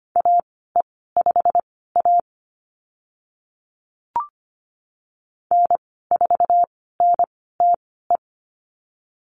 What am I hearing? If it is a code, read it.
Morse code, heard as a keyed tone, about 25 wpm, spaced as for 16 wpm, 700 Hz. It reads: AE5A N4NTE
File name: Lesson 15g - Number 5 - Call Signs 25wpm